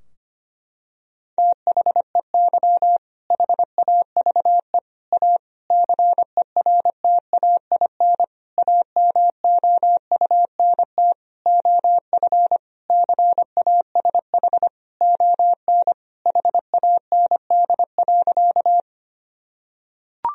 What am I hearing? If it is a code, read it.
Morse code, heard as a keyed tone, about 25 wpm, 700 Hz. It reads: THEY HAVE A CERTAIN AMOUNT OF CASH ON HAND.
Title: Sentences from Top 1000 Words - Review Part 4 - 25wpm